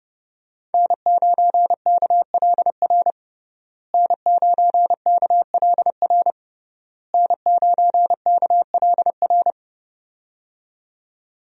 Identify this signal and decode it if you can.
Morse code, heard as a keyed tone, about 30 wpm, 700 Hz. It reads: N9KLR N9KLR N9KLR
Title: USA Call Signs - 5 Characters - Repeated 3x - 30wpm